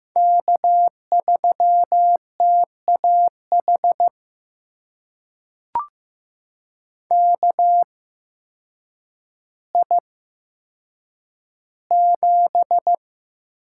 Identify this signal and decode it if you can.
Morse code, heard as a keyed tone, about 15 wpm, spaced as for 8 wpm, 700 Hz. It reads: K3TAH KI7
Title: USA Call Signs - 5 Characters - 8x Character Spacing - 15wpm